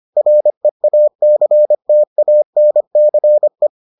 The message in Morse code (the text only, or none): REACTANCE